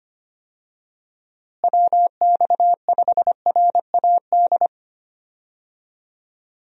Morse code, heard as a keyed tone, about 25 wpm, 700 Hz. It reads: WX5RAD
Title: USA Call Signs - 6 Characters - Repeated 3x - 25wpm